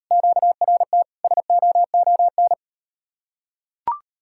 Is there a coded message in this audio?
Yes: Morse code from 0.1 to 2.5 s, heard as a keyed tone, about 38 wpm, 700 Hz. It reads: QRT SOON